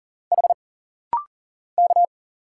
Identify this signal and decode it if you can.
Morse code, heard as a keyed tone, about 40 wpm, 700 Hz. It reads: H K
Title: Single Letters - Rapid-Fire 40wpm